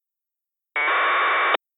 Packet radio (AX.25, as in APRS).